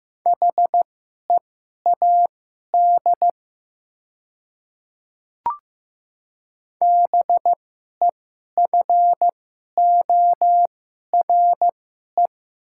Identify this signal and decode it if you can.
Morse code, heard as a keyed tone, about 15 wpm, spaced as for 11 wpm, 700 Hz. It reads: HEAD BEFORE
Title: 2x Character Spacing - Top 500 Words - 15wpm